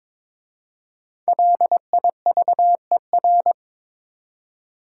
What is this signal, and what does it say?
Morse code, heard as a keyed tone, about 22 wpm, 700 Hz. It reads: LIVER
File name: 5 Letter Words - First 3 Letters Given - 22wpm